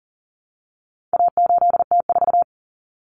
Morse code, heard as a keyed tone, about 40 wpm, 700 Hz. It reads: A8T4
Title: Common Cut Numbers - 4-Digit 40wpm